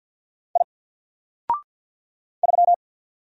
Morse code, heard as a keyed tone, about 48 wpm, 700 Hz. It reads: I 3
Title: Single Letter-Number Alt 50wpm